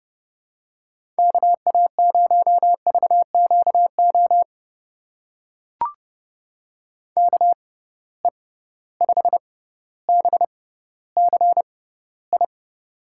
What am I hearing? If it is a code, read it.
Morse code, heard as a keyed tone, about 30 wpm, spaced as for 15 wpm, 700 Hz. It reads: KA0VQO KE5BCI
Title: USA Call Signs - 6 Characters - 6x Character Spacing - 30wpm